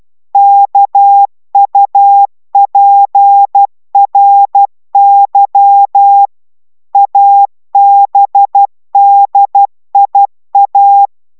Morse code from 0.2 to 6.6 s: KUPRY